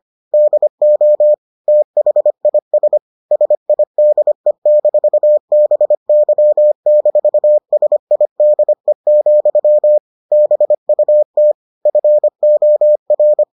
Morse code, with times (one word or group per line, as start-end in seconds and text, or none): 0.3-1.3 DO
1.7-3.0 THIS
3.3-10.0 SIDE-BY-SIDE,
10.3-11.5 BUT
11.9-13.4 FOR